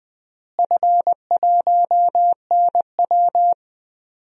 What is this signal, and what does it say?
Morse code, heard as a keyed tone, about 21 wpm, 700 Hz. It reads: F1NW
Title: DXCC-100 Call Signs - 4 Characters - Repeated 2x - 20wpm